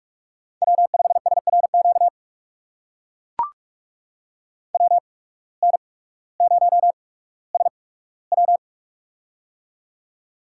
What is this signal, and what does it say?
Morse code, heard as a keyed tone, about 44 wpm, spaced as for 19 wpm, 700 Hz. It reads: W5SRQ WN0SW
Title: USA Call Signs - 5 Characters - 8x Character Spacing - 45wpm